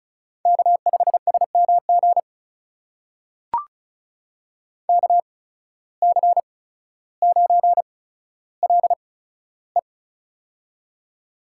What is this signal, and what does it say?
Morse code, heard as a keyed tone, about 34 wpm, spaced as for 15 wpm, 700 Hz. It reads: K5SMG KC9LE
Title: USA Call Signs - 5 Characters - 8x Character Spacing - 35wpm